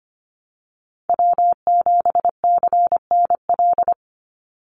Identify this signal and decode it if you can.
Morse code, heard as a keyed tone, about 25 wpm, 700 Hz. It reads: W7CNL
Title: NA SOTA Chaser Callsigns 25wpm